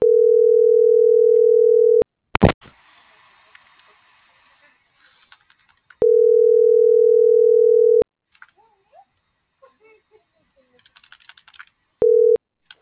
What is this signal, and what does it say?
Ringback tone (North America), 440 + 480 Hz.